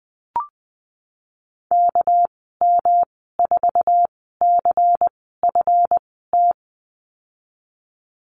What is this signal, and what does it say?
Morse code, heard as a keyed tone, about 20 wpm, spaced as for 15 wpm, 700 Hz. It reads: KM4CFT
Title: 207 Lesson 33g - Letter j - Call Signs 20wpm